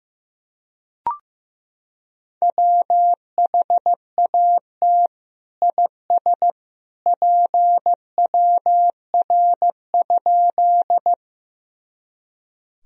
Morse code, heard as a keyed tone, about 15 wpm, 700 Hz. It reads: WHAT IS PWR?
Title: Sentences from Top 500 Words and Abbreviations 15wpm